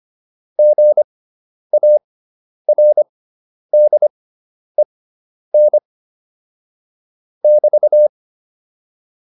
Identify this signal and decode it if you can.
Morse code, heard as a keyed tone, about 25 wpm, spaced as for 10 wpm, 600 Hz. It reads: GARDEN <BT>
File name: QSO 410 10